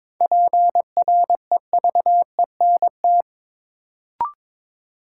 Morse code, heard as a keyed tone, about 22 wpm, 700 Hz. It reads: PREVENT
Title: Sentences from Top 1100 Words - New Word Drill Alt - 22wpm